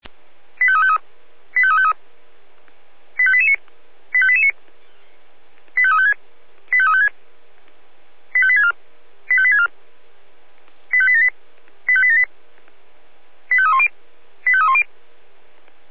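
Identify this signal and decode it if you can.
Five-tone selective call (ZVEI), heard as quick runs of five tones, tones of about 70 ms. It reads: 85353 85353 85889 85889 85356 85356 85864 85864 85878 85878 85310 85310
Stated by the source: ZVEI five-tone selective call 25420 received via SDR.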